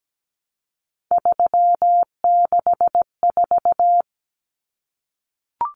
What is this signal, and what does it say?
Morse code, heard as a keyed tone, about 17 wpm, 700 Hz. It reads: 364
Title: Common Cut Numbers - 3-Digit 17wpm